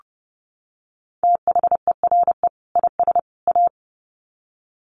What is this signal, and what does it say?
Morse code, heard as a keyed tone, about 30 wpm, 700 Hz. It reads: THERE IS A
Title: Sets of 3 Words 30wpm